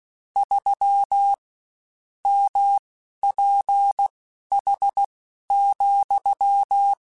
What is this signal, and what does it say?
Morse code, heard as a keyed tone, about 16 wpm, spaced as for 11 wpm, 800 Hz. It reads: 3 MPH,
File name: Spaced Characters / Spaced Characters-mp3s / Weather 5 5 sp @15